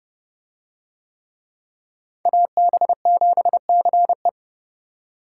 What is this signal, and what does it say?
Morse code, heard as a keyed tone, about 28 wpm, 700 Hz. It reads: AB7CE